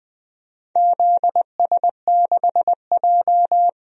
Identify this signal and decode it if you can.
Morse code, heard as a keyed tone, about 20 wpm, 700 Hz. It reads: ZS6J
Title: DXCC-100 Call Signs - 4 Characters - 20wpm